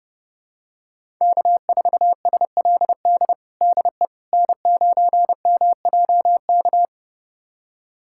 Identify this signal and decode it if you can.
Morse code, heard as a keyed tone, about 32 wpm, 700 Hz. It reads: K4SLD DE N9MJK
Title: USA Callsign DE Callsign - 5 Characters - 30wpm